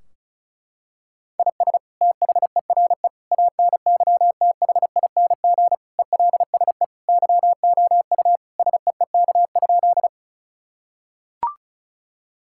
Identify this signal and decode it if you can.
Morse code, heard as a keyed tone, about 36 wpm, 700 Hz. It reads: IS THERE ANYTHING ELSE YOU SEEK?
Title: Sentences from Top 1600 Words 35wpm